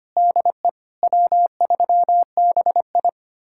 Morse code, heard as a keyed tone, about 30 wpm, 700 Hz. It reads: DE W3BI
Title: Speed-Racing Common QSO Phrases Encore - 25wpm